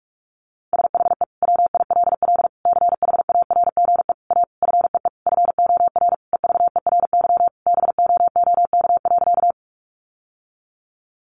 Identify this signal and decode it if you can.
Morse code, heard as a keyed tone, about 46 wpm, 700 Hz. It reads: SHE WILL CHARGE A FEE FOR EVERY BOOK.
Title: Sentences from Top 1500 Words 45wpm